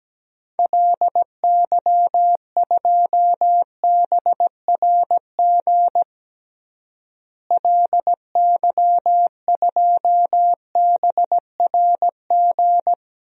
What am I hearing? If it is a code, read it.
Morse code, heard as a keyed tone, about 17 wpm, 700 Hz. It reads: LY2BRG LY2BRG
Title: DXCC-100 Call Signs - 6 Characters - Repeated 3x - 17wpm